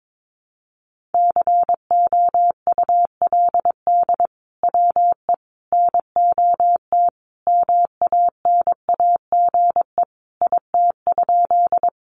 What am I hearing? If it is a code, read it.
Morse code, heard as a keyed tone, about 22 wpm, 700 Hz. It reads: COULD WE NOT MANAGE IT?
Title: Sentences from Top 1400 Words 22wpm